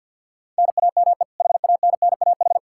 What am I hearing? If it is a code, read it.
Morse code, heard as a keyed tone, about 46 wpm, 700 Hz. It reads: NAME HANNAH
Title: Practice QSO - Exchange 1 - Beginning - 50wpm